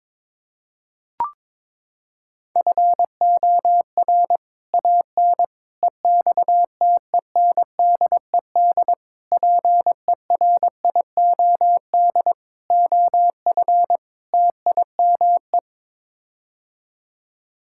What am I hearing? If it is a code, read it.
Morse code, heard as a keyed tone, about 21 wpm, 700 Hz. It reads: FOR AN EXTENDED PERIOD OF TIME
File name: practice / 22wpm / Sets of 6 Words - 22wpm